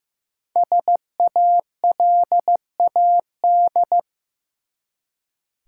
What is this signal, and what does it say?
Morse code, heard as a keyed tone, about 15 wpm, 700 Hz. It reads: SALAD
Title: Five Letter Words - 15wpm